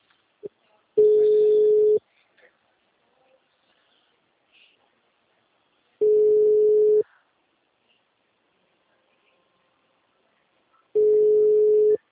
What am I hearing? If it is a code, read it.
Ringback tone (Europe, 425 Hz).
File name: out- -2007- - - .21907